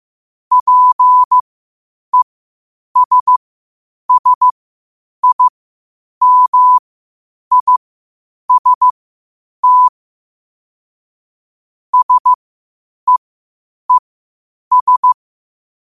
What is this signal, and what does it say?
Morse code, heard as a keyed tone, about 15 wpm, spaced as for 8 wpm, 1000 Hz. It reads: PESSIMIST SEES
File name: qotdfiles / 8wpm